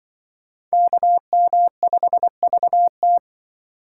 Morse code, heard as a keyed tone, about 24 wpm, 700 Hz. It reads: KM5VT
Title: Speed Racing - USA 2x2 Call Signs 20wpm